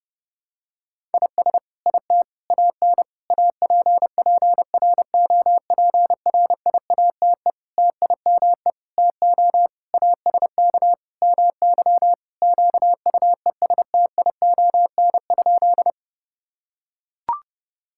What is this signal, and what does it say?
Morse code, heard as a keyed tone, about 32 wpm, 700 Hz. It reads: IS IT AN APPROPRIATE TIME TO ASK MY QUESTION?
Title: Sentences from Top 1000 Words - Review Part 5 - 30wpm